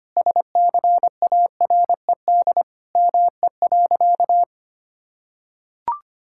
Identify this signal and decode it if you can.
Morse code, heard as a keyed tone, about 25 wpm, 700 Hz. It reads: SCARED ME.
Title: Sentences from Top 1000 Words - Review Part 5 - 25wpm